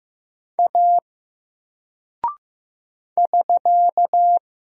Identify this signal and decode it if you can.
Morse code, heard as a keyed tone, about 15 wpm, 700 Hz. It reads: A <SK>
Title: Single Character (Letter:Number:Punctuation:Prosign) 15wpm